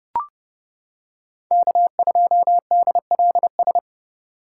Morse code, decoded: K2DLS

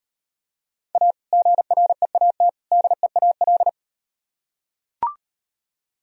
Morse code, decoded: A GREAT DEAL

